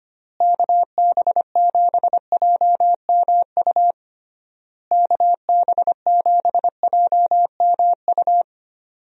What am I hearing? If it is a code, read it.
Morse code, heard as a keyed tone, about 25 wpm, 700 Hz. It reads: KB7JMU KB7JMU
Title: USA Call Signs - 6 Characters - Repeated 3x - 25wpm